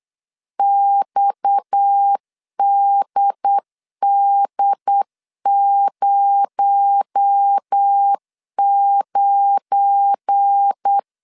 Morse code, heard as a keyed tone, about 8 wpm, 800 Hz. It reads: XDD09
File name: CW / DET / 1 9 1500C S B 8wpm 5624